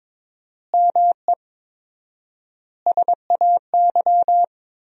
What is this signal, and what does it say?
Morse code, heard as a keyed tone, about 22 wpm, 700 Hz. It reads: ME SAY
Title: 4x Word-Spacing - Sentences from Top 300 Words - 22wpm